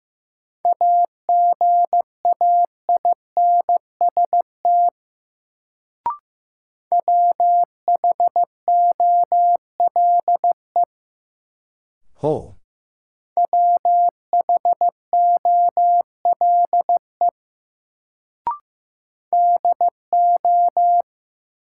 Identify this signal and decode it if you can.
Morse code, heard as a keyed tone, about 15 wpm, 700 Hz. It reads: AGAINST WHOLE WHOLE DO
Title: Sets of 2 Words - New Word Drill - 15wpm